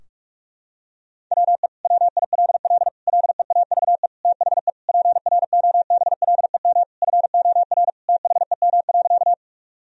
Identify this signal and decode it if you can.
Morse code, heard as a keyed tone, about 46 wpm, 700 Hz. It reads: WE WILL LEAVE THE PROBLEM FOR THEM.
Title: Sentences from Top 1000 Words - Review Part 2 - 45wpm